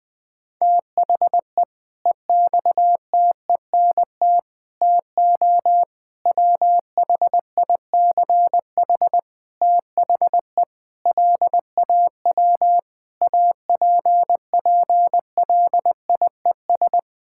Morse code, heard as a keyed tone, about 20 wpm, 700 Hz. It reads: THE EXTENT TO WHICH THE LAW APPLIES
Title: Sentences with 5-Word Phrases - 20wpm